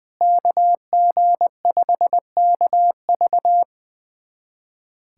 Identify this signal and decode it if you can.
Morse code, heard as a keyed tone, about 21 wpm, 700 Hz. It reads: KG5KV
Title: USA Callsign DE Callsign - 5 Characters - 20wpm